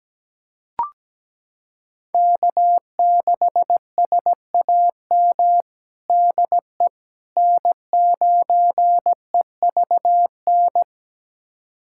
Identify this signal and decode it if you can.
Morse code, heard as a keyed tone, about 17 wpm, 700 Hz. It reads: K6SAM DE N9EVN
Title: USA Callsign DE Callsign - 5 Characters - 17wpm